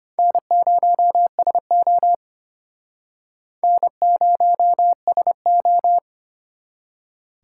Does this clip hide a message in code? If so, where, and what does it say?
Yes: Morse code from 0.2 to 6.0 s, heard as a keyed tone, about 31 wpm, 700 Hz. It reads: N0SO N0SO